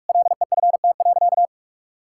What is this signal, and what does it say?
Morse code, heard as a keyed tone, about 46 wpm, 700 Hz. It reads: LEFT.
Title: Sentences from Top 1600 Words 45wpm